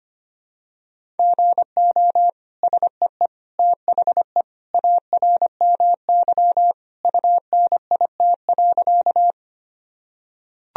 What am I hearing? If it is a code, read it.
Morse code, heard as a keyed tone, about 25 wpm, 700 Hz. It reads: GO SEE THE ARMY UNIT.
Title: Sentences from Top 1000 Words - Review Part 2 - 25wpm